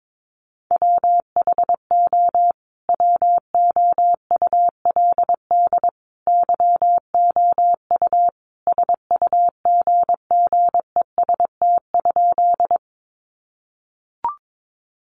Morse code, heard as a keyed tone, about 22 wpm, 700 Hz. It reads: WHO WOULD YOU SUGGEST?
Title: Sentences from Top 1300 Words 22wpm